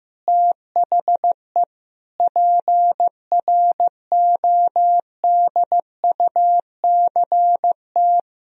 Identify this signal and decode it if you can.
Morse code, heard as a keyed tone, about 16 wpm, 700 Hz. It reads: THE PRODUCT
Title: Sentences from Top 1000 Words - Review Part 2 - 15wpm